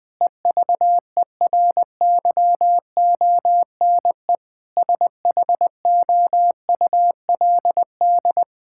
Morse code, heard as a keyed tone, about 20 wpm, 700 Hz. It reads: EVERYONE SHOULD
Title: Sentences from Top 1000 Words - Review Part 5 - 20wpm